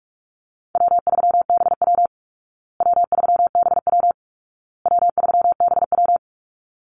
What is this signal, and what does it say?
Morse code, heard as a keyed tone, about 44 wpm, 700 Hz. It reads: W3BW W3BW W3BW
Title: Easy Call Signs Repeated 3x - 45wpm